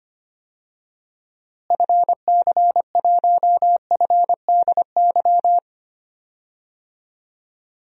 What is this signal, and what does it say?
Morse code, heard as a keyed tone, about 25 wpm, 700 Hz. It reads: FC1FDY
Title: DXCC-100 Call Signs - 6 Characters - Repeated 3x - 25wpm